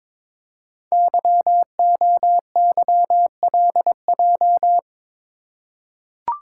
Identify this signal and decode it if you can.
Morse code, heard as a keyed tone, about 22 wpm, 700 Hz. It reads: YOYLJ
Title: Code Groups - 5 Letter:Numbers - 22wpm